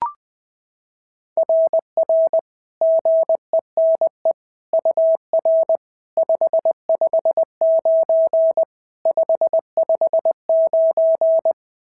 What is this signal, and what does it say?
Morse code, heard as a keyed tone, about 21 wpm, 690 Hz. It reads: RR GENE UR 559 559
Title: SOTA QSO Practice 20wpm